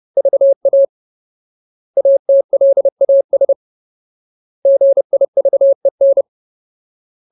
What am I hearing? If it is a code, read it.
Morse code, heard as a keyed tone, about 30 wpm, 550 Hz. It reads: VA ATLAS GIVEN
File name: Ultimate Word List / 30WPM Ultimate